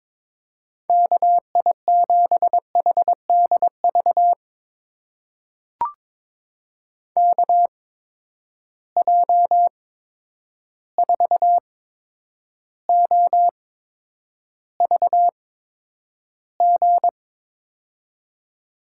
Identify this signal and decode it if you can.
Morse code, heard as a keyed tone, about 22 wpm, spaced as for 9 wpm, 700 Hz. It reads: KI7HDV KJ4OVG
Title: USA Call Signs - 6 Characters - 8x Character Spacing - 22wpm